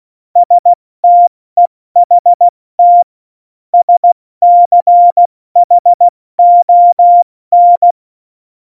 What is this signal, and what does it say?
Morse code, heard as a keyed tone, about 16 wpm, spaced as for 14 wpm, 700 Hz. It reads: STEHT SCHON
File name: Morsekurs / Witze / Witz 13. November 2018, 70 F80